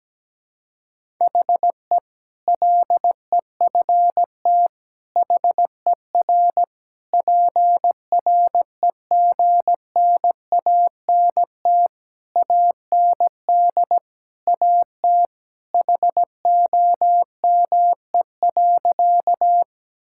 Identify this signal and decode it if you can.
Morse code, heard as a keyed tone, about 17 wpm, 700 Hz. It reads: HE LEFT HER PREGNANT AND AT HOME.
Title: Sentences from Top 1000 Words - Review Part 6 - 17wpm